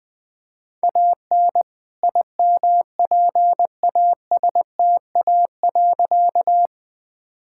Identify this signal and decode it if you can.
Morse code, heard as a keyed tone, about 20 wpm, 700 Hz. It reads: AN IMPASTA.
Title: All-Age Appropriate Jokes - 20wpm